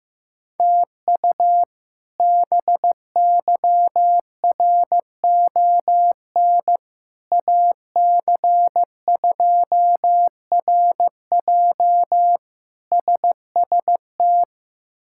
Morse code, heard as a keyed tone, about 15 wpm, 700 Hz. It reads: TU BYRON AC2RJ SST
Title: K1USN SST QSO Practice - 15wpm